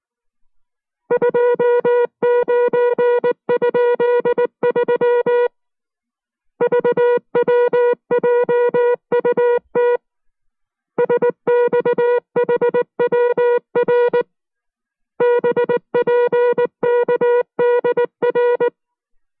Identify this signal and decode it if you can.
Morse code, heard as a keyed tone, about 19 wpm, 480 Hz. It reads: 29?3 VWJUT SXHWR BPKDR